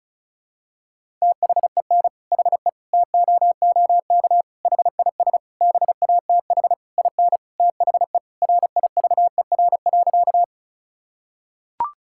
Morse code, heard as a keyed tone, about 36 wpm, 700 Hz. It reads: THEN HE TOOK HIS BATH IN THE RIVER.
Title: Sentences from Top 1600 Words 35wpm